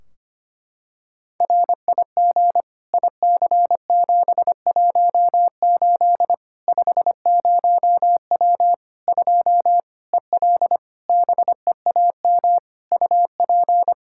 Morse code, heard as a keyed tone, about 25 wpm, 700 Hz. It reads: RIG IC718 50W 2 EL BEAM UP
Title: K1USN SST QSO Practice 25wpm